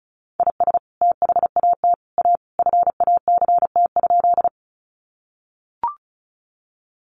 Morse code, heard as a keyed tone, about 34 wpm, 700 Hz. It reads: IS THAT A FACT?
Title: Sentences from Top 500 Words 35wpm